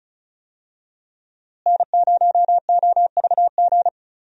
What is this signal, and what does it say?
Morse code, heard as a keyed tone, about 34 wpm, 700 Hz. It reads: N0OVG
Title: USA Call Signs - 5 Characters - Repeated 3x - 35wpm